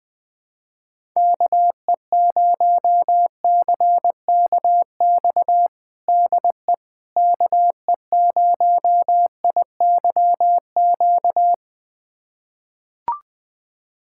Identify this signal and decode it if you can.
Morse code, heard as a keyed tone, about 20 wpm, 700 Hz. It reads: KE0CKX DE KE0IYQ
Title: USA Callsign DE Callsign - 6 Characters - 20wpm